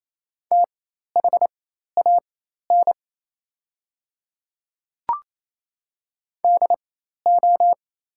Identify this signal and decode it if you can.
Morse code, heard as a keyed tone, about 28 wpm, spaced as for 13 wpm, 700 Hz. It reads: THAN DO
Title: 4x Character Spacing - Top 100 Words - 28wpm